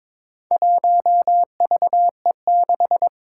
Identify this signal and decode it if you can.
Morse code, heard as a keyed tone, about 22 wpm, 700 Hz. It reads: 1VE6
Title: Four Challenging Letter-Number Combinations 22wpm